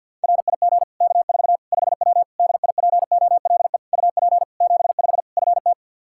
Morse code, heard as a keyed tone, about 53 wpm, 700 Hz. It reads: RIG K4 5W DIPOLE UP 75 FT
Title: Practice QSO - Exchange 2 - Info - 50wpm